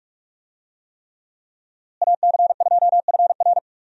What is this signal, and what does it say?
Morse code, heard as a keyed tone, about 46 wpm, 700 Hz. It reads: AC2FR
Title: USA Call Signs - 5 Characters - Repeated 3x - 45wpm